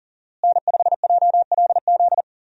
Morse code, heard as a keyed tone, about 40 wpm, 700 Hz. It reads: N5JLZ
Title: USA Call Signs - 5 Characters - Repeated 2x - 40wpm